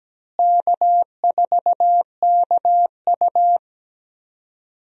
Morse code, heard as a keyed tone, about 17 wpm, 700 Hz. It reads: K4KU
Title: USA Call Signs - 4 Characters - Repeated 3x - 17wpm